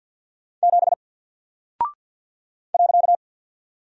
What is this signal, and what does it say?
Morse code, heard as a keyed tone, about 48 wpm, 700 Hz. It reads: 7 .
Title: Single Character (Letter:Number:Punctuation:Prosign) Alt 50wpm